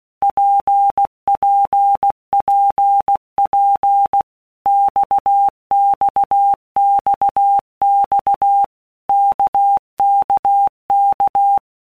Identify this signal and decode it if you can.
Morse code, heard as a keyed tone, about 16 wpm, 800 Hz. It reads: PPPP XXXX KKK
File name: Basic Letter Groups By Blocks / Groups@15 / RKPX@15